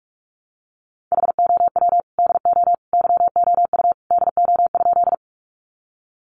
Morse code, heard as a keyed tone, about 44 wpm, 700 Hz. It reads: HOW DO YOU DO?
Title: Sentences from Top 100 Words 45wpm